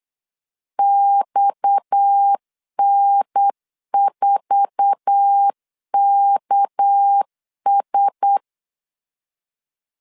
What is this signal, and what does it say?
Morse code, heard as a keyed tone, about 8 wpm, 800 Hz. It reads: XN4KS
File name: LEZIONI / 8 YJ45 / 8 YJ45 8wpm 3214